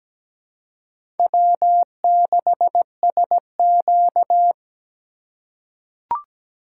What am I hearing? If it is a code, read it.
Morse code, heard as a keyed tone, about 17 wpm, 700 Hz. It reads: W6SQ
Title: USA Call Signs - 4 Characters - 2x Character Spacing - 17wpm